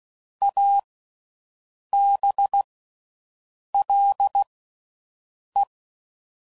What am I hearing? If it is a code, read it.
Morse code, heard as a keyed tone, about 16 wpm, spaced as for 6 wpm, 800 Hz. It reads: ABLE